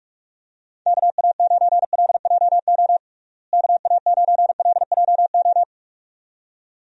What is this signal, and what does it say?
Morse code, heard as a keyed tone, about 46 wpm, 700 Hz. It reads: KA9LJO KA9LJO
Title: USA Call Signs - 6 Characters - Repeated 2x - 45wpm